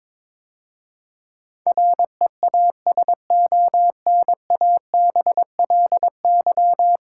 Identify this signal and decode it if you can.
Morse code, heard as a keyed tone, about 22 wpm, 700 Hz. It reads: REASONABLY
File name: practice / 22wpm / 10 Letter Words - First 3 Letters Given - 22wpm